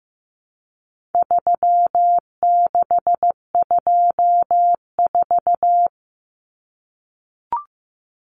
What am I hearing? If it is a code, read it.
Morse code, heard as a keyed tone, about 15 wpm, 700 Hz. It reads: 3624
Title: Common Cut Numbers - 4-Digit 15wpm